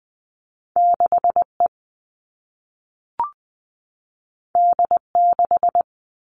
Morse code, heard as a keyed tone, about 20 wpm, 700 Hz. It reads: 6 E D 6
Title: All Cut Numbers - 2-Digit 20wpm